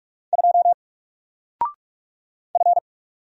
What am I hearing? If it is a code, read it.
Morse code, heard as a keyed tone, about 44 wpm, 700 Hz. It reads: 2 F
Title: Single Letter-Number Alt 45wpm